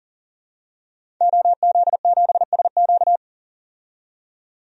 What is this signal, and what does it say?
Morse code, heard as a keyed tone, about 40 wpm, 700 Hz. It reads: OZ7SQ
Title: DXCC-100 Call Signs - 5 Characters - 40wpm